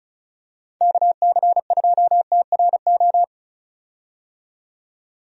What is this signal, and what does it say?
Morse code, heard as a keyed tone, about 36 wpm, 700 Hz. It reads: KC2TRO
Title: USA Call Signs - 6 Characters - 35wpm